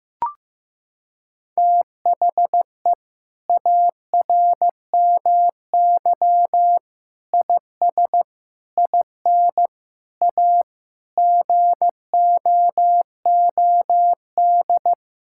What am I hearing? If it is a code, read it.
Morse code, heard as a keyed tone, about 15 wpm, 700 Hz. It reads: THE ARMY IS IN A GOOD
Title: Sentences from Top 1000 Words - Review Part 6 - 15wpm